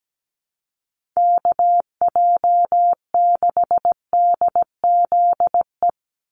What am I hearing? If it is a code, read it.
Morse code, heard as a keyed tone, about 17 wpm, 700 Hz. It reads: KJ6DZE
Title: USA Call Signs 17wpm